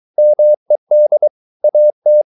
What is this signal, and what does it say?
Morse code, heard as a keyed tone, about 23 wpm, 600 Hz. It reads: MED AT
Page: Ferie 23 wpm